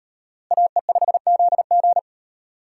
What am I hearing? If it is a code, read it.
Morse code, heard as a keyed tone, about 36 wpm, 700 Hz. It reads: AE5ZG